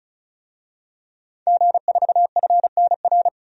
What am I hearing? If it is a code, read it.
Morse code, heard as a keyed tone, about 34 wpm, 700 Hz. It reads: G4FNR
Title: DXCC-100 Call Signs - 5 Characters - 35wpm